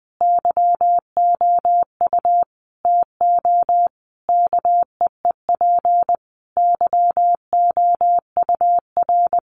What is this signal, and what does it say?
Morse code, heard as a keyed tone, about 20 wpm, 700 Hz. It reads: YOU TO KEEP YOUR
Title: Sentences from Top 1100 Words 20wpm